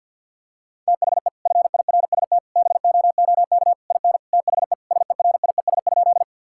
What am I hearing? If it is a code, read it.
Morse code, heard as a keyed tone, about 50 wpm, 700 Hz. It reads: THE FIRST BOOK IN THE SERIES?
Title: Sentences from Top 1000 Words - Review Part 5 - 50wpm